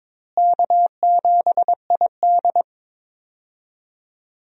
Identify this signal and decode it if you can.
Morse code, heard as a keyed tone, about 22 wpm, 700 Hz. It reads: K7ID
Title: USA Call Signs - 4 Characters - Repeated 2x - 22wpm